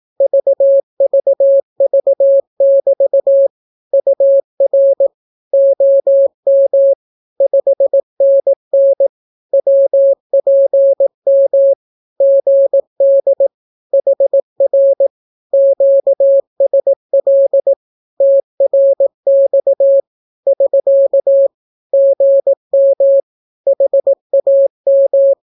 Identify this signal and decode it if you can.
Morse code, heard as a keyed tone, about 18 wpm, 550 Hz. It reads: VVV<BT> UR OM 5NN WPM GD HR QSL TRX <SK> GM HAM
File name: naukapodstaw / skroty / skroty